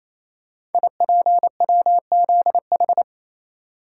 Morse code, heard as a keyed tone, about 29 wpm, 690 Hz. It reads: IPWZH
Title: Code Groups - 5 Letters - 28wpm